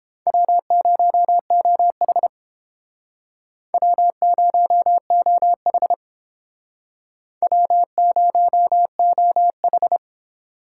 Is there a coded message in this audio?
Yes: Morse code from 0.3 to 10.0 s, heard as a keyed tone, about 29 wpm, 700 Hz. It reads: W0OH W0OH W0OH